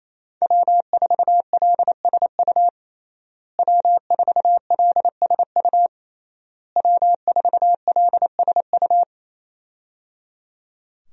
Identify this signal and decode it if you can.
Morse code, heard as a keyed tone, about 28 wpm, 700 Hz. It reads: W4LSU W4LSU W4LSU
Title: USA Call Signs - 5 Characters - Repeated 3x - 28wpm